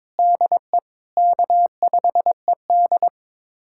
Morse code, heard as a keyed tone, about 22 wpm, 700 Hz. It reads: DE K5ED
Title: USA Callsign DE Callsign - 4 Characters - 22wpm